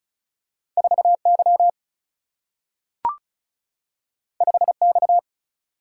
Morse code, heard as a keyed tone, about 36 wpm, 700 Hz. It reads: 4Y 5X
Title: Single-Letter Single-Number 35wpm